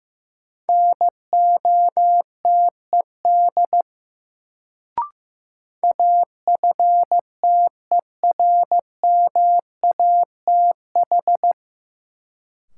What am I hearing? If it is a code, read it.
Morse code, heard as a keyed tone, about 15 wpm, 700 Hz. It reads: NOTED AFTERMATH
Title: Sets of 5 Words - New Word Drill - 15wpm